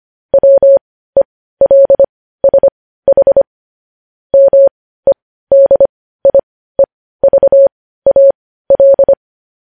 Morse code, heard as a keyed tone, about 25 wpm, spaced as for 15 wpm, 550 Hz. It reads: WELSH MEDIEVAL
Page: News Headlines in Morse Code at 15 WPM